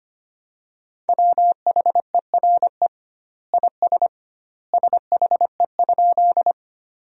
Morse code, heard as a keyed tone, about 25 wpm, 700 Hz. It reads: WHERE IS SHE?
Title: 2x Word-Spacing - Sentences from Top 200 Words - 25wpm